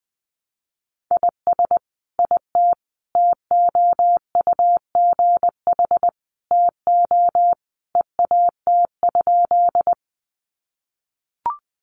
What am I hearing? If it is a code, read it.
Morse code, heard as a keyed tone, about 20 wpm, 700 Hz. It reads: IS IT TOUGH TO EAT?
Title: Sentences from Top 1200 Words 20wpm